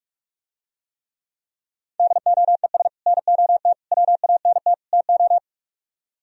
Morse code, heard as a keyed tone, about 50 wpm, 700 Hz. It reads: DOES NOT WANT TO